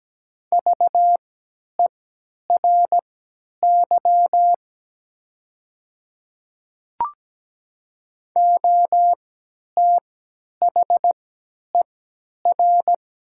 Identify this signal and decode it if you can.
Morse code, heard as a keyed tone, about 17 wpm, spaced as for 10 wpm, 700 Hz. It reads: VERY OTHER
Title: 3x Character Spacing - Top 200 Words - 17wpm